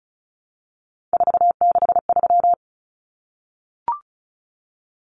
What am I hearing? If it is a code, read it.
Morse code, heard as a keyed tone, about 34 wpm, 700 Hz. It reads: 463
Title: Common Cut Numbers - 3-Digit 35wpm